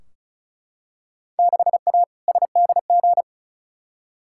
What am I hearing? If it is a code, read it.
Morse code, heard as a keyed tone, about 36 wpm, 700 Hz. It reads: 6A SDG
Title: Field Day Exchanges Encore - 35wpm